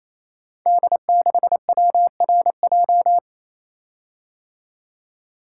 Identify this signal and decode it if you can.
Morse code, heard as a keyed tone, about 27 wpm, 700 Hz. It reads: D6WRJ
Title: Code Groups - 5 Letter:Numbers - 28wpm